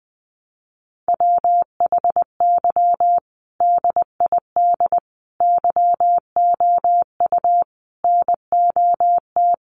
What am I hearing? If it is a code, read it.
Morse code, heard as a keyed tone, about 20 wpm, 700 Hz. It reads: WHY DID YOU NOT
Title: Sentences from Top 1100 Words 20wpm